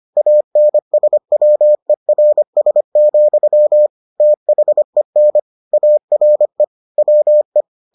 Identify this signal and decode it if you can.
Morse code, heard as a keyed tone, about 25 wpm, 600 Hz. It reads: ANSWERS, THEN ARE WE